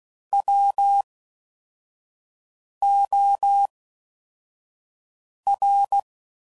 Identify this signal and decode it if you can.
Morse code, heard as a keyed tone, about 16 wpm, spaced as for 4 wpm, 800 Hz. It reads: WOR